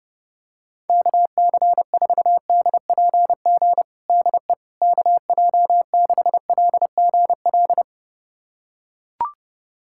Morse code, heard as a keyed tone, about 30 wpm, 700 Hz. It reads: KC4DPG DE KJ6LGL
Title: USA Callsign DE Callsign - 6 Characters - 30wpm